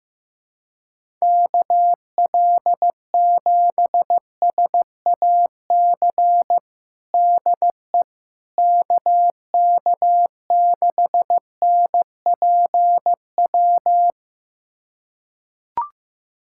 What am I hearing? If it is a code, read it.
Morse code, heard as a keyed tone, about 15 wpm, 700 Hz. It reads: KL7SAC DE KK6NPW
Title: USA Callsign DE Callsign - 6 Characters - 15wpm